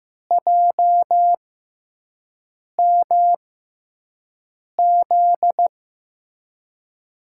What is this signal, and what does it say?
Morse code, heard as a keyed tone, about 15 wpm, spaced as for 5 wpm, 700 Hz. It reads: JMZ
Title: USA Call Signs - 6 Characters - 6x Character Spacing - 15wpm